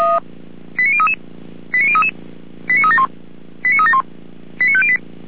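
Five-tone selective call (ZVEI), heard as quick runs of five tones, tones of about 70 ms. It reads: 89033 89033 89371 89371 89508
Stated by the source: ZVEI five-tone selective call received by SDR.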